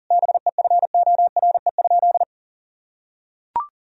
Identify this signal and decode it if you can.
Morse code, heard as a keyed tone, about 40 wpm, 700 Hz. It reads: BEFORE?
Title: Sentences from Top 1000 Words - Review Part 3 - 40wpm